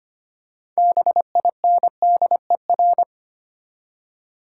Morse code, heard as a keyed tone, about 25 wpm, 690 Hz. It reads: BINDER
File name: practice / 25wpm / Easy Words with Hints - 25wpm